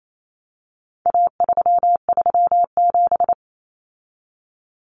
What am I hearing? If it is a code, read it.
Morse code, heard as a keyed tone, about 28 wpm, 700 Hz. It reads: A337
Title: Common Cut Numbers - 4-Digit 28wpm